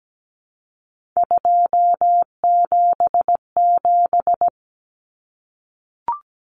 Morse code, heard as a keyed tone, about 17 wpm, 700 Hz. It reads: 277
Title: Common Cut Numbers - 3-Digit 17wpm